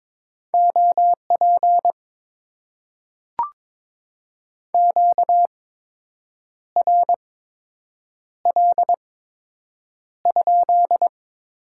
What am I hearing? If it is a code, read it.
Morse code, heard as a keyed tone, about 22 wpm, spaced as for 7 wpm, 700 Hz. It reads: OP QRL?
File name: Top 90 QSO Elements - 8x Character Spacing - 22wpm